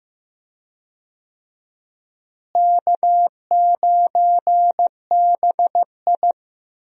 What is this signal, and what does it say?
Morse code, heard as a keyed tone, about 15 wpm, 700 Hz. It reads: K9BI
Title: USA Call Signs - 4 Characters - Repeated 2x - 15wpm